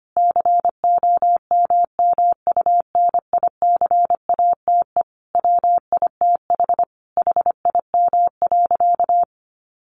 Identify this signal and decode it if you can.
Morse code, heard as a keyed tone, about 25 wpm, 700 Hz. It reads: COMMUNICATE WITH HIM.
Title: Sentences from Top 1300 Words 25wpm